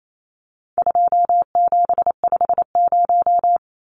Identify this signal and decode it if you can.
Morse code, heard as a keyed tone, about 28 wpm, 700 Hz. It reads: 2750
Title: Numbers - 4-Digit 28wpm